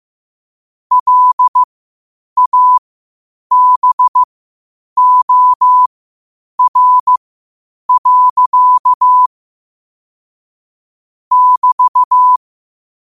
Morse code, heard as a keyed tone, about 15 wpm, spaced as for 8 wpm, 1000 Hz. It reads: LABOR. <BT>
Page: qotd-at-8-wpm